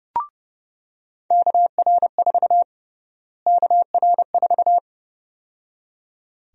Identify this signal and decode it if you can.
Morse code, heard as a keyed tone, about 31 wpm, 700 Hz. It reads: KR4 KR4
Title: USA Call Sign Prefixes - Repeated 2x - 30wpm